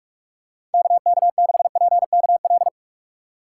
Morse code, heard as a keyed tone, about 46 wpm, 700 Hz. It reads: KK6PKL
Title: USA Call Signs - 3x Character Spacing - 45wpm